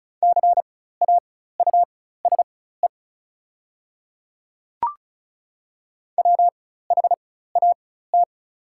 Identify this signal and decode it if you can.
Morse code, heard as a keyed tone, about 34 wpm, spaced as for 16 wpm, 700 Hz. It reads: CAUSE WHAT
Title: 4x Character Spacing - Top 300 Words - 35wpm